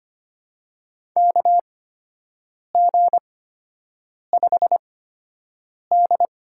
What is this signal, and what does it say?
Morse code, heard as a keyed tone, about 25 wpm, spaced as for 7 wpm, 700 Hz. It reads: KG5D
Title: USA Call Signs - 6 Characters - 8x Character Spacing - 25wpm